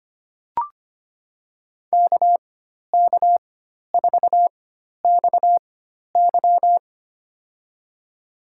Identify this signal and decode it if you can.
Morse code, heard as a keyed tone, about 25 wpm, spaced as for 12 wpm, 700 Hz. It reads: KK4XY
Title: USA Call Signs - 5 Characters - 4x Character Spacing - 25wpm